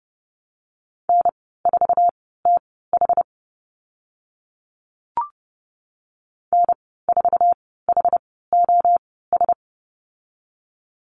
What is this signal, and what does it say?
Morse code, heard as a keyed tone, about 31 wpm, spaced as for 17 wpm, 690 Hz. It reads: N4TH N4HOS
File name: Lesson 15g - Number 5 - Call Signs 30wpm